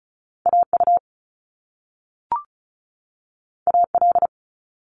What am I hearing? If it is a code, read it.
Morse code, heard as a keyed tone, about 36 wpm, 700 Hz. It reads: AU AL
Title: Top Two Letter Combinations in Words 35wpm